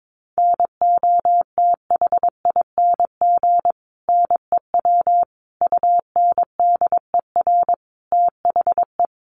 Morse code, heard as a keyed tone, about 22 wpm, 700 Hz. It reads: NOTHING NEW UNDER THE
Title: Sentences from Top 500 Words 22wpm